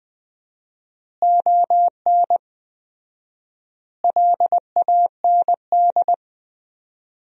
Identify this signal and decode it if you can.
Morse code, heard as a keyed tone, about 20 wpm, 700 Hz. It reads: ON LAND
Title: 4x Word-Spacing - Sentences from Top 500 Words - 20wpm